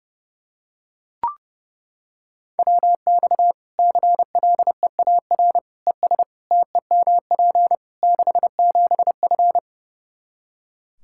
Morse code, heard as a keyed tone, about 30 wpm, 700 Hz. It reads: WX CLEAR ES TEMP 67F
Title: K1USN SST QSO Practice 30wpm